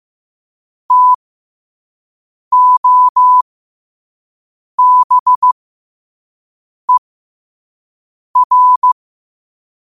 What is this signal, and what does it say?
Morse code, heard as a keyed tone, about 15 wpm, spaced as for 5 wpm, 1000 Hz. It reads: TOBER
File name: qotdfiles / 5wpm